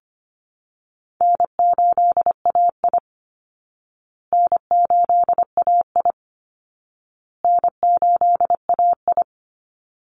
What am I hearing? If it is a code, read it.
Morse code, heard as a keyed tone, about 25 wpm, 700 Hz. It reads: N8AI N8AI N8AI
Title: Easy Call Signs Repeated 3x - 25wpm